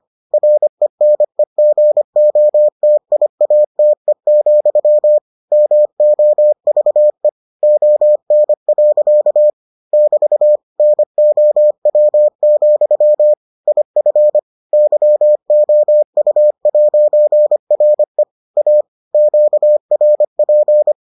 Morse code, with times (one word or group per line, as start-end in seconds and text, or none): 0.3-5.2 RENEGOTIATE,
5.5-7.3 MOVE
7.6-9.5 ON.
9.9-10.6 <BT>
10.8-13.3 NOW,
13.7-14.4 IF
14.7-18.2 YOU'RE
18.6-18.8 A
19.1-20.9 QRP